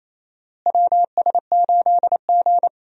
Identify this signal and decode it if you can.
Morse code, heard as a keyed tone, about 28 wpm, 700 Hz. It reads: WS8G
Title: USA Call Signs - 4 Characters - Repeated 3x - 28wpm